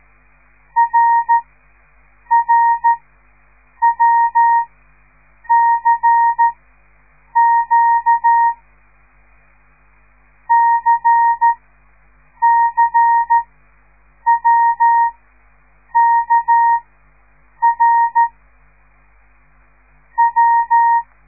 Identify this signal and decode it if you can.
Morse code, heard as a keyed tone, about 13 wpm, spaced as for 7 wpm, 930 Hz. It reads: RRWCQ CCWKR W
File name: radiomorningstar / morse / les 5